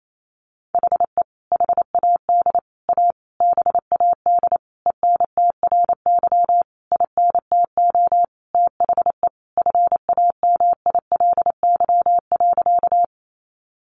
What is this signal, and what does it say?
Morse code, heard as a keyed tone, about 28 wpm, 700 Hz. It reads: HE HAD A BAD ENTRY INTO THE FAMILY.
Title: Sentences from Top 1500 Words 28wpm